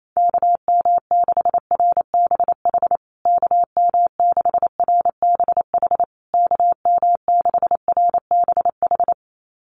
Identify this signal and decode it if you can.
Morse code, heard as a keyed tone, about 28 wpm, 690 Hz. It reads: KM6RBH KM6RBH KM6RBH
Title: USA Call Signs Repeated 3x 28wpm